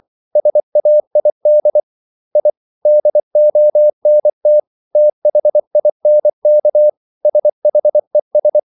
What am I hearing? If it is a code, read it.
Morse code, heard as a keyed tone, about 24 wpm, 600 Hz. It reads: SAID I DONT THINK SHES